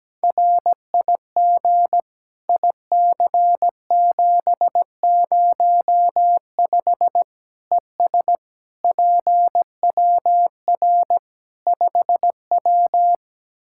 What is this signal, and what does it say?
Morse code, heard as a keyed tone, about 17 wpm, 700 Hz. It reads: RIG IC705 ES PWR 5W
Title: Practice QSO - Exchange 2 - Info - 17wpm